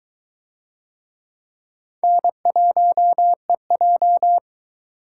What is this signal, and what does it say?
Morse code, heard as a keyed tone, about 21 wpm, 700 Hz. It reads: N1EJ